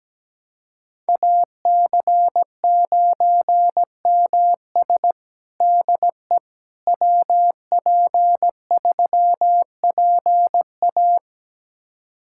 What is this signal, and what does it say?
Morse code, heard as a keyed tone, about 17 wpm, 700 Hz. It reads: AC9MS DE WP3PA
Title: USA Callsign DE Callsign - 5 Characters - 17wpm